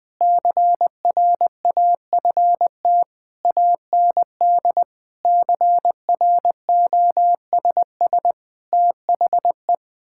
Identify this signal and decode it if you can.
Morse code, heard as a keyed tone, about 21 wpm, 700 Hz. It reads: CRAFT AND CROSS THE
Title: Sentences from Top 1000 Words - Review Part 6 - 20wpm